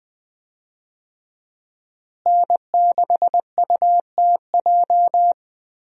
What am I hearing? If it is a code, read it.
Morse code, heard as a keyed tone, about 20 wpm, 700 Hz. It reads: N6UTJ
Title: USA Call Signs - 5 Characters - Repeated 2x - 20wpm